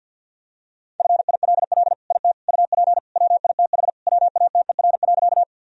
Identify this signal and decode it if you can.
Morse code, heard as a keyed tone, about 52 wpm, 700 Hz. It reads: FILL IT UP WITH WATER.
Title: Sentences from Top 1000 Words - Review Part 3 - 50wpm